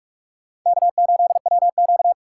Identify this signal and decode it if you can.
Morse code, heard as a keyed tone, about 41 wpm, 700 Hz. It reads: K8WQ